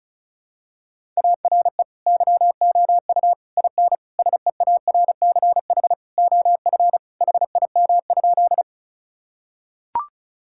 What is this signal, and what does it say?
Morse code, heard as a keyed tone, about 36 wpm, 700 Hz. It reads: ARE YOU IN SEARCH OF HIM?
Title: Sentences from Top 1000 Words - Review Part 5 - 35wpm